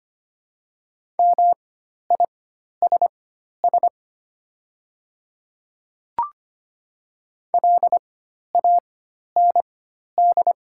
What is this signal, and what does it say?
Morse code, heard as a keyed tone, about 25 wpm, spaced as for 12 wpm, 700 Hz. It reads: MISS LAND
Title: 4x Character Spacing - Top 500 Words - 25wpm